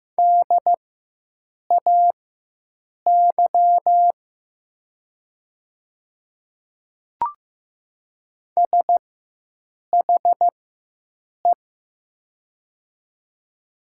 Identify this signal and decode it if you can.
Morse code, heard as a keyed tone, about 15 wpm, spaced as for 7 wpm, 700 Hz. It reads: DAY SHE
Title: 4x Character Spacing - Top 300 Words - 15wpm